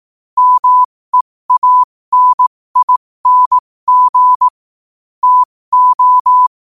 Morse code, heard as a keyed tone, about 18 wpm, spaced as for 15 wpm, 1000 Hz. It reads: MEANING TO
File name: qotdfiles / 15wpm